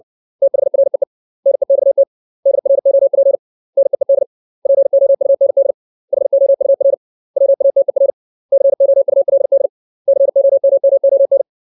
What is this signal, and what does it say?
Morse code, heard as a keyed tone, about 60 wpm, 550 Hz. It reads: THREE NEXT DROP NEED POUND HOUR WATER COULD COMMON
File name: 632 words / 632 Words 60WPM